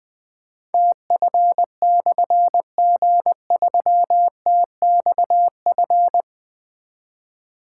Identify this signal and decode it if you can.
Morse code, heard as a keyed tone, about 20 wpm, 700 Hz. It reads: TF/G3TXF
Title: Difficult Call Signs 20wpm